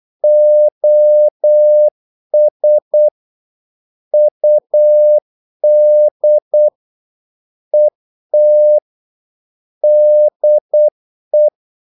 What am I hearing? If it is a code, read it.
Morse code, heard as a keyed tone, about 8 wpm, 600 Hz. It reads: OS UD ET DE
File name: Korte ord DK 8wpm